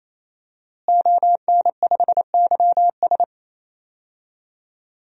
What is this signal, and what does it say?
Morse code, heard as a keyed tone, about 28 wpm, 700 Hz. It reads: ON5YS
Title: DXCC-100 Call Signs - 5 Characters - Repeated 3x - 28wpm